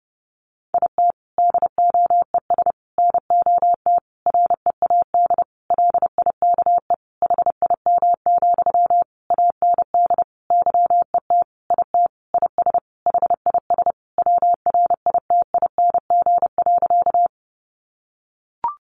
Morse code, heard as a keyed tone, about 28 wpm, 700 Hz. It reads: IT DOES NOT READ LIKE HIM, AND YET IT IS HIS WRITING.
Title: Sentences from Top 800 Words 30wpm